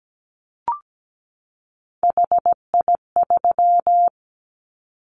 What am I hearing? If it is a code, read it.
Morse code, heard as a keyed tone, about 17 wpm, 700 Hz. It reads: HI3
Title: Callsign Prefixes for Top 100 DXCC - 17wpm